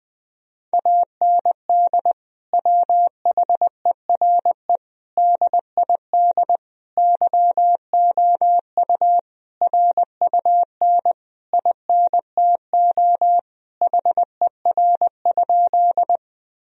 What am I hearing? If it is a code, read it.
Morse code, heard as a keyed tone, about 20 wpm, 700 Hz. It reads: AND WHERE DID YOU RUN INTO HER?
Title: Sentences from Top 1000 Words - Review Part 1 - 20wpm